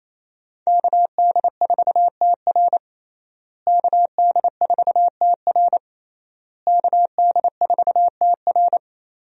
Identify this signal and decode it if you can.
Morse code, heard as a keyed tone, about 27 wpm, 700 Hz. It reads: KD4TR KD4TR KD4TR
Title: USA Call Signs - 5 Characters - Repeated 3x - 28wpm